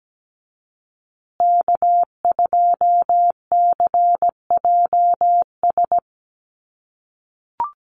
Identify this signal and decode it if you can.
Morse code, heard as a keyed tone, about 17 wpm, 700 Hz. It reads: K2CJS
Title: USA Call Signs Repeated 3x 17wpm